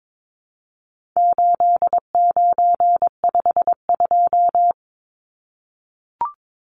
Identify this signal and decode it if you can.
Morse code, heard as a keyed tone, about 22 wpm, 700 Hz. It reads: 8952
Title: Numbers - 4-Digit 22wpm